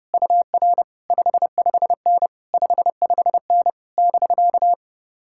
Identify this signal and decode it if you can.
Morse code, heard as a keyed tone, about 33 wpm, 700 Hz. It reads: UR 55N 55N <BK>